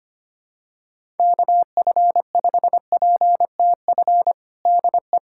Morse code, heard as a keyed tone, about 25 wpm, 700 Hz. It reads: KF5PTF DE
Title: USA Callsign DE Callsign - 6 Characters - 25wpm